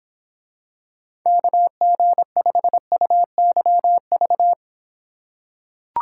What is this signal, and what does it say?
Morse code, heard as a keyed tone, about 26 wpm, 700 Hz. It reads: KG5UYV